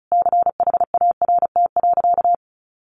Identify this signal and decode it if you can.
Morse code, heard as a keyed tone, about 34 wpm, 700 Hz. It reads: CHART.
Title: Sentences from Top 1300 Words 35wpm